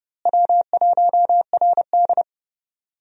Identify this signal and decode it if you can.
Morse code, heard as a keyed tone, about 29 wpm, 700 Hz. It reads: W1RD
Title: USA Call Signs - 4 Characters - Repeated 2x - 30wpm